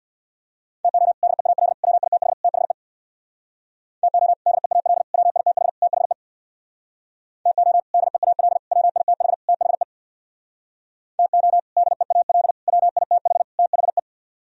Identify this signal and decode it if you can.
Morse code, heard as a keyed tone, about 62 wpm, 700 Hz. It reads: TO DEAL WITH THE TO DEAL WITH THE TO DEAL WITH THE TO DEAL WITH THE
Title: Speed-Racing - Sets of 4 Words 50wpm